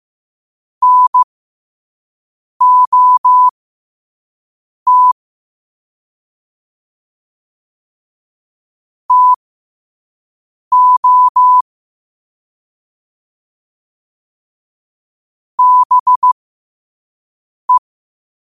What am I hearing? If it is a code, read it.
Morse code, heard as a keyed tone, about 15 wpm, spaced as for 5 wpm, 1000 Hz. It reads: NOT TO BE
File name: qotdfiles / 5wpm